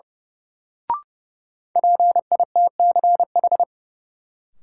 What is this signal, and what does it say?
Morse code, heard as a keyed tone, about 32 wpm, 700 Hz. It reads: PITCH
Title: Sentences from Top 1300 Words - New Word Drill - 30wpm